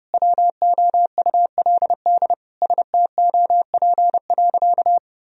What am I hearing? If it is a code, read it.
Morse code, heard as a keyed tone, about 30 wpm, 700 Hz. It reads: WOULD STOP.
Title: Sentences from Top 1000 Words - Review Part 1 - 30wpm